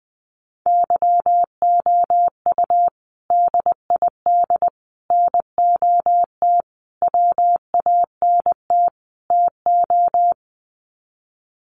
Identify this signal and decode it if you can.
Morse code, heard as a keyed tone, about 19 wpm, 700 Hz. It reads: YOU DID NOT WANT TO
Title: Sets of 5 Words Encore2 20wpm